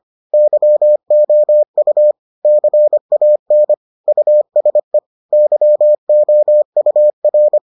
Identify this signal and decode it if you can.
Morse code, heard as a keyed tone, about 25 wpm, 600 Hz. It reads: YOU CAN USE YOUR